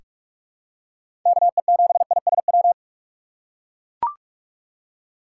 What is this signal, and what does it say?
Morse code, heard as a keyed tone, about 43 wpm, 700 Hz. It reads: KE7ISW
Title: USA Call Signs - 6 Characters - 3x Character Spacing - 45wpm